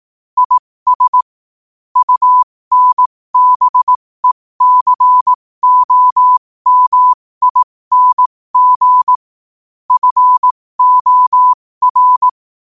Morse code, heard as a keyed tone, about 18 wpm, spaced as for 15 wpm, 1000 Hz. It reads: IS UNBECOMING FOR
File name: qotdfiles / 15wpm